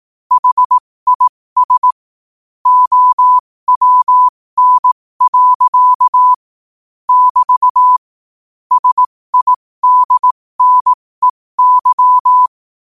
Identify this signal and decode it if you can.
Morse code, heard as a keyed tone, about 18 wpm, spaced as for 15 wpm, 1000 Hz. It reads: HIS OWN. <BT> SIDNEY
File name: qotdfiles / 15wpm